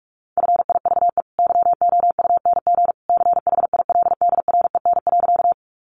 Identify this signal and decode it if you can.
Morse code, heard as a keyed tone, about 44 wpm, 700 Hz. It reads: FIVE YOUNG CHILDREN.
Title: Sentences from Top 400 Words 45wpm